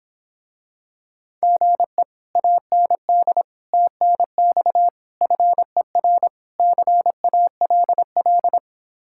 Morse code, heard as a keyed tone, about 26 wpm, 700 Hz. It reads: GE AND TNX FER CALL